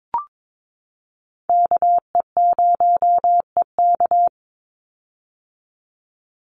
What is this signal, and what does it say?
Morse code, heard as a keyed tone, about 22 wpm, 700 Hz. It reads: KE0EK
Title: Call Signs - 22wpm